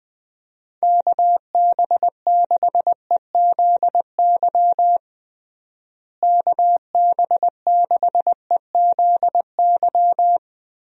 Morse code, heard as a keyed tone, about 20 wpm, 700 Hz. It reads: KB6EZY KB6EZY
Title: USA Call Signs - 6 Characters - Repeated 3x - 20wpm